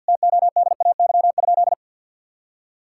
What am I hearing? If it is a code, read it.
Morse code, heard as a keyed tone, about 48 wpm, 700 Hz. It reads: TODAY?
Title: Sentences from Top 1000 Words - Review Part 5 - 50wpm